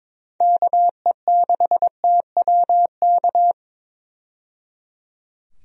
Morse code, heard as a keyed tone, about 22 wpm, 700 Hz. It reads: KE6TWK
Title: USA Call Signs - 6 Characters - 22wpm